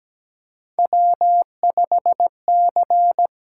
Morse code, heard as a keyed tone, about 17 wpm, 700 Hz. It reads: W5C
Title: USA Call Signs - 6x Character Spacing - 17wpm